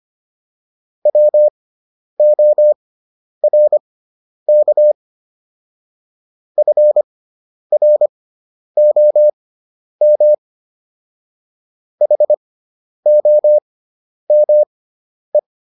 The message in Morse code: WORK FROM HOME

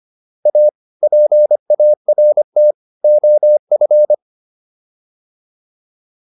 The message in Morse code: A PART OF

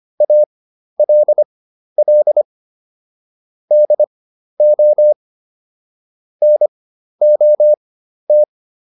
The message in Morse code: ALL DO NOT

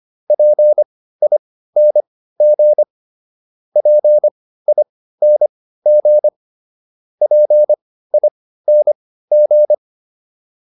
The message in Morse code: PING PING PING

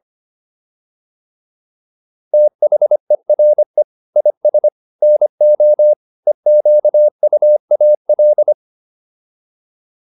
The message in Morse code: THERE IS NO EQUAL

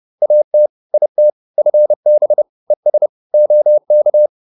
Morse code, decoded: AT IT FB ES OK

